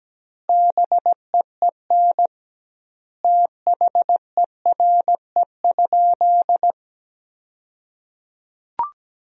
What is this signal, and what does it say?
Morse code, heard as a keyed tone, about 17 wpm, 700 Hz. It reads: BEEN THERE?
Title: 2x Word-Spacing - Sentences from Top 100 Words - 17wpm